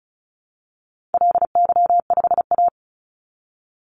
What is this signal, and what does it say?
Morse code, heard as a keyed tone, about 34 wpm, 700 Hz. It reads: LY5A
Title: Call Signs Repeated 2x - 35wpm